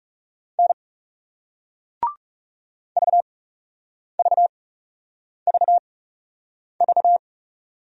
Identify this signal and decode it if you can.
Morse code, heard as a keyed tone, about 39 wpm, spaced as for 8 wpm, 700 Hz. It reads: N VVVV